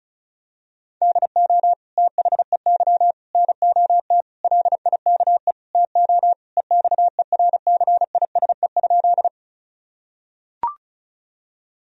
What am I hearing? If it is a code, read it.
Morse code, heard as a keyed tone, about 36 wpm, 700 Hz. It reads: DO THEY NOT LIKE TO EXERCISE?
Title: Sentences from Top 1000 Words - Review Part 4 - 35wpm